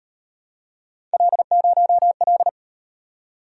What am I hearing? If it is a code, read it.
Morse code, heard as a keyed tone, about 38 wpm, 700 Hz. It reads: L0L